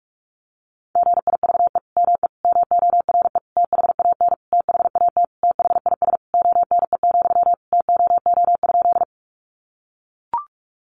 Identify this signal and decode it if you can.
Morse code, heard as a keyed tone, about 44 wpm, 700 Hz. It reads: GIVE ME MORE THAN THAT THIS ONE, TOO?
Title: Sentences from Top 200 Words 45wpm